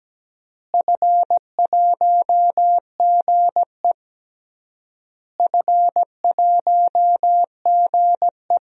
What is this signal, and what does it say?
Morse code, heard as a keyed tone, about 17 wpm, 700 Hz. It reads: F1GE F1GE
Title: DXCC-100 Call Signs - 4 Characters - Repeated 3x - 17wpm